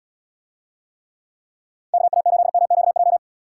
Morse code, heard as a keyed tone, about 72 wpm, 700 Hz. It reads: KA8MQJ